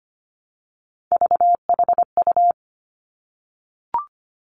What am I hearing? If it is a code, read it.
Morse code, heard as a keyed tone, about 25 wpm, 700 Hz. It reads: VHU
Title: Three Challenging Letter-Number Combinations 25wpm